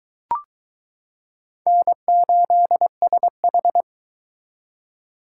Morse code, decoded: N8SH

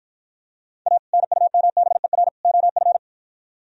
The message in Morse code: A NUMBER OF

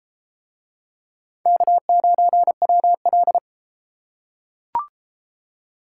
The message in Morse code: K9WL

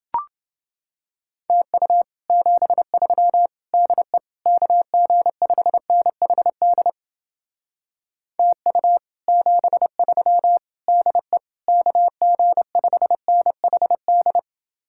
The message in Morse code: TU 73 DE KG5NHD TU 73 DE KG5NHD

